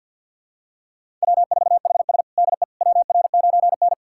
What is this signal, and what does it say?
Morse code, heard as a keyed tone, about 52 wpm, 700 Hz. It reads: W4HS DE WR9N
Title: USA Callsign DE Callsign - 4 Characters - 50wpm